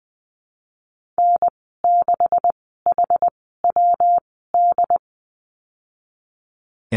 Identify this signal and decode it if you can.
Morse code, heard as a keyed tone, about 19 wpm, spaced as for 14 wpm, 700 Hz. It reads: N6HWD
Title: 147 Lesson 24g - Letter y - Call Signs 20wpm